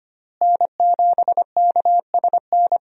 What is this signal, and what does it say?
Morse code, heard as a keyed tone, about 25 wpm, 700 Hz. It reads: N7KSN
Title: USA Call Signs - 5 Characters - Repeated 2x - 25wpm